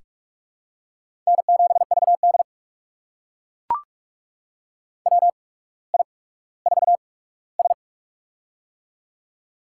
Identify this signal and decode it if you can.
Morse code, heard as a keyed tone, about 44 wpm, spaced as for 19 wpm, 700 Hz. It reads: N7VD WI4S
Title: USA Call Signs - 4 Characters - 8x Character Spacing - 45wpm